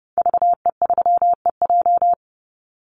Morse code, heard as a keyed tone, about 30 wpm, 700 Hz. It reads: VE3EJ
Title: Call Signs Repeated 3x - 30wpm